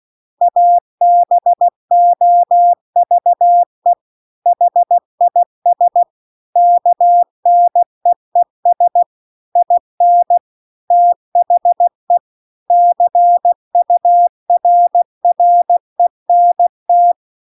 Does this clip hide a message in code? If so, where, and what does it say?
Yes: Morse code from 0.4 to 17.1 s, heard as a keyed tone, about 16 wpm, 700 Hz. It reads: ABOVE HIS KNEES IN THE CURRENT